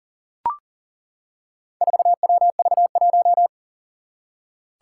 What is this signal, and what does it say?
Morse code, heard as a keyed tone, about 40 wpm, 700 Hz. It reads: 4WV1
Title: Four Challenging Letter-Number Combinations 40wpm